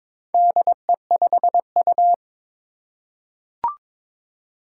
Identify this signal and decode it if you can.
Morse code, heard as a keyed tone, about 22 wpm, 700 Hz. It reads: DE5U
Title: Four Challenging Letter-Number Combinations 22wpm